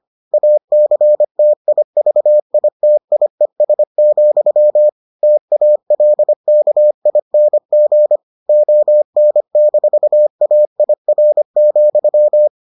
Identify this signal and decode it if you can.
Morse code, heard as a keyed tone, about 25 wpm, 600 Hz. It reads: ACTIVITIES, TALKING ON-AIR,